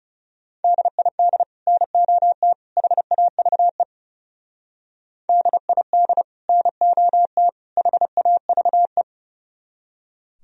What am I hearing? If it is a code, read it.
Morse code, heard as a keyed tone, about 34 wpm, 700 Hz. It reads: DID NOT HAVE DID NOT HAVE
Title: Speed-Racing - Sets of 3 Words 30wpm